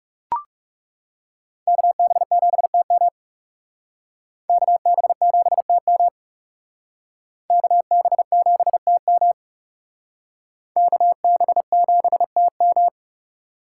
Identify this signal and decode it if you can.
Morse code, heard as a keyed tone, about 38 wpm, 700 Hz. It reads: KB7TM KB7TM KB7TM KB7TM